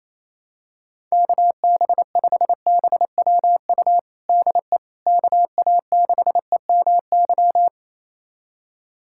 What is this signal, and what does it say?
Morse code, heard as a keyed tone, about 28 wpm, 700 Hz. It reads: KB5BWU DE KA6EMY
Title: USA Callsign DE Callsign - 6 Characters - 28wpm